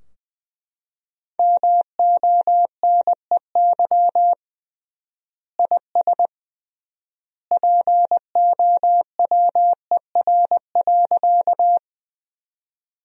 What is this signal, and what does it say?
Morse code, heard as a keyed tone, about 20 wpm, 700 Hz. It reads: MONEY IS POWER.
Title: 3x Word-Spacing - Sentences from Top 500 Words - 20wpm